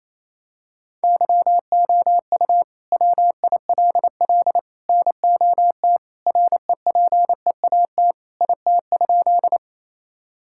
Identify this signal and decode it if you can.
Morse code, heard as a keyed tone, about 28 wpm, 700 Hz. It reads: YOU WILL NOT REPEAT IT?
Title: Sentences from Top 1600 Words 28wpm